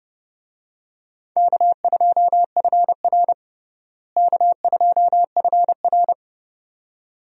Morse code, heard as a keyed tone, about 30 wpm, 700 Hz. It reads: K2FR K2FR
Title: USA Call Signs - 4 Characters - Repeated 2x - 30wpm